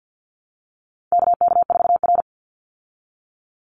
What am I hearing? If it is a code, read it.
Morse code, heard as a keyed tone, about 50 wpm, 700 Hz. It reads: KK4R
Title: Call Signs Repeated 2x - 50wpm